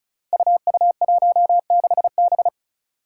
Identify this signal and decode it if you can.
Morse code, heard as a keyed tone, about 36 wpm, 700 Hz. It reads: UU16B
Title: Five Challenging Letter-Number Combinations 35wpm